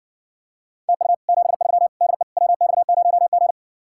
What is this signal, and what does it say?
Morse code, heard as a keyed tone, about 60 wpm, 700 Hz. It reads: TU 73 DE WX0G